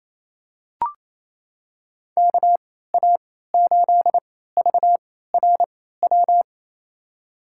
Morse code, heard as a keyed tone, about 28 wpm, spaced as for 16 wpm, 700 Hz. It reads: KA8VRW
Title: USA Call Signs - 6 Characters - 3x Character Spacing - 28wpm